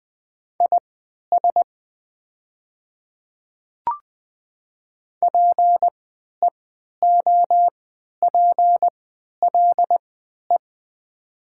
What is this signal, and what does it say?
Morse code, heard as a keyed tone, about 20 wpm, spaced as for 11 wpm, 700 Hz. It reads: IS PEOPLE
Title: 3x Character Spacing - Top 100 Words - 20wpm